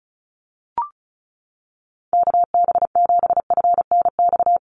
Morse code, heard as a keyed tone, about 34 wpm, 700 Hz. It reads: KB7FNX
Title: USA Call Signs Repeated 2x 35wpm